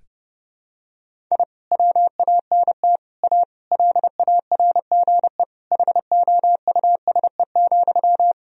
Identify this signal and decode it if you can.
Morse code, heard as a keyed tone, about 30 wpm, 700 Hz. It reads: I WANT A LARGE HOUSE,
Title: Sentences from Top 1000 Words - Review Part 1 - 30wpm